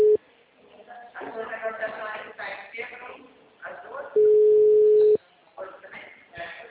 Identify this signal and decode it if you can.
Ringback tone (Europe, 425 Hz), with speech.